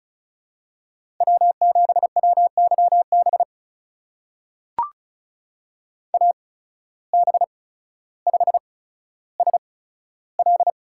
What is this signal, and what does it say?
Morse code, heard as a keyed tone, about 36 wpm, spaced as for 15 wpm, 700 Hz. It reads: W7WYB AB5SL
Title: USA Call Signs - 5 Characters - 8x Character Spacing - 35wpm